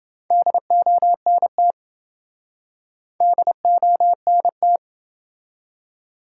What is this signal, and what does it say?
Morse code, heard as a keyed tone, about 27 wpm, 700 Hz. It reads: DONT DONT